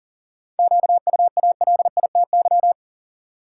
Morse code, heard as a keyed tone, about 40 wpm, 700 Hz. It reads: QUALITY
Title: Sentences from Top 1000 Words - Review Part 4 - New Word Drill - 40wpm